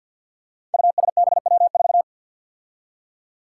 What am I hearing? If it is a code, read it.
Morse code, heard as a keyed tone, about 47 wpm, 700 Hz. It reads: USBW4
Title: Five Challenging Letter-Number Combinations 50wpm